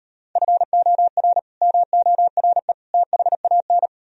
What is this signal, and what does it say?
Morse code, heard as a keyed tone, about 38 wpm, 700 Hz. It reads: FOR MORE THAN